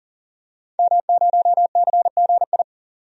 Morse code, heard as a keyed tone, about 40 wpm, 700 Hz. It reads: M0CGI
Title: DXCC-100 Call Signs - 5 Characters - 40wpm